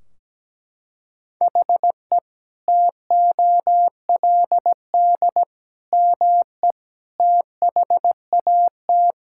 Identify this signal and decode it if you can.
Morse code, heard as a keyed tone, about 17 wpm, 700 Hz. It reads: HE TOLD ME THAT
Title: Sentences from Top 1600 Words 17wpm